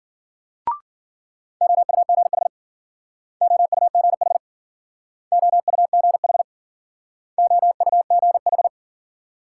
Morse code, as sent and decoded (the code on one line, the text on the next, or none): --- ..- --. .... / --- ..- --. .... / --- ..- --. .... / --- ..- --. ....
OUGH OUGH OUGH OUGH